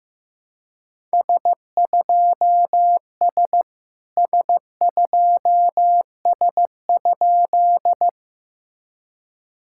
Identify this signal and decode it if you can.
Morse code, heard as a keyed tone, about 17 wpm, 700 Hz. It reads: S2S S2S?
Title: SOTA QSO Practice 15wpm